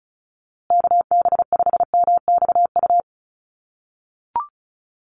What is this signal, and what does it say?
Morse code, heard as a keyed tone, about 34 wpm, 700 Hz. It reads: KB5MXU
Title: USA Call Signs Repeated 2x 35wpm